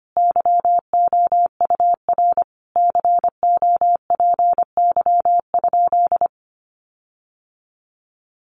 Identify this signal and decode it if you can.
Morse code, heard as a keyed tone, about 25 wpm, 700 Hz. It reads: YOUR COPY?
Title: Sentences from Top 1400 Words 25wpm